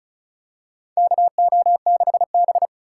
Morse code, heard as a keyed tone, about 36 wpm, 700 Hz. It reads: KO6B
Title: USA Call Signs - 4 Characters - 2x Character Spacing - 35wpm